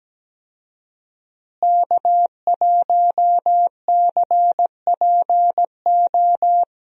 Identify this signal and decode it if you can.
Morse code, heard as a keyed tone, about 17 wpm, 700 Hz. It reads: K1CPO
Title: USA Call Signs - 5 Characters - Repeated 2x - 17wpm